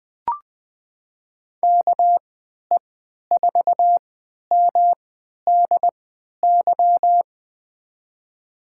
Morse code, heard as a keyed tone, about 20 wpm, spaced as for 11 wpm, 700 Hz. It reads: KE4MDY
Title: USA Call Signs - 3x Character Spacing - 20wpm